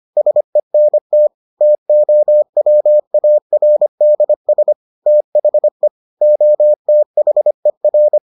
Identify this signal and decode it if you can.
Morse code, heard as a keyed tone, about 25 wpm, 600 Hz. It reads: SENT TOWARDS THE OTHER